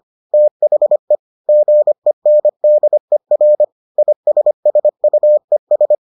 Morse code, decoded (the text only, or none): THE GENDER ISSUES